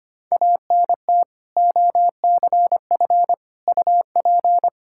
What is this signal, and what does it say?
Morse code, heard as a keyed tone, about 25 wpm, 700 Hz. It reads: ANT OCF UP
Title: Practice QSO - Exchange 2 - Info - 25wpm